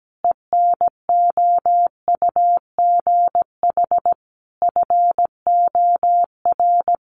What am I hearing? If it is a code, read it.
Morse code, heard as a keyed tone, about 17 wpm, 700 Hz. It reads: ENOUGH FOR
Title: Sentences from Top 1400 Words 17wpm